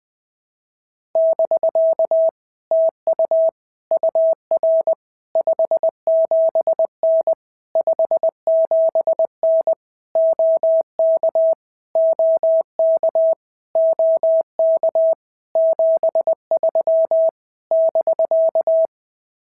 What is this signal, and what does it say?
Morse code, heard as a keyed tone, about 20 wpm, 650 Hz. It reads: <BK> TU UR 57N 57N OK OK OK 73 <BK>
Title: POTA QSO Practice - 20wpm